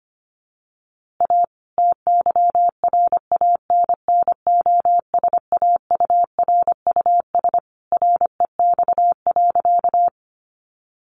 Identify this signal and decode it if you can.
Morse code, heard as a keyed tone, about 25 wpm, 700 Hz. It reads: A TYRANNOSAURUS REX.
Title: A Sound Of Thunder 25wpm